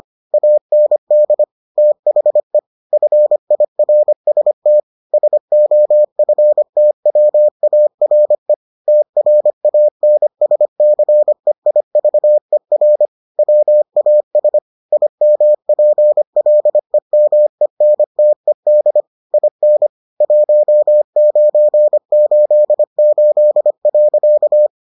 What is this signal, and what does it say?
Morse code, heard as a keyed tone, about 25 wpm, 600 Hz. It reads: AND THE FIRST SOFTWARE TRANSCEIVER WAS IMPLEMENTED IN 1988.